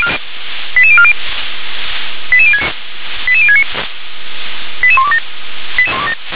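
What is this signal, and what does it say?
Five-tone selective call (ZVEI), heard as quick runs of five tones, tones of about 70 ms. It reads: 88040 88069 88127
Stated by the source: ZVEI five-tone selective call received by SDR.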